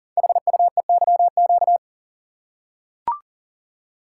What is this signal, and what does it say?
Morse code, heard as a keyed tone, about 40 wpm, 700 Hz. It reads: HUEYQ
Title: Code Groups - 5 Letters - 40wpm